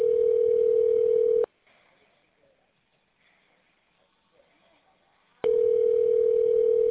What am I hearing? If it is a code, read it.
Ringback tone (North America), 440 + 480 Hz.